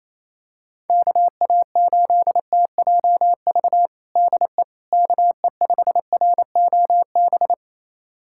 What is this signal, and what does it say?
Morse code, heard as a keyed tone, about 28 wpm, 700 Hz. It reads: KA8TJV DE KE5ROB
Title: USA Callsign DE Callsign 28wpm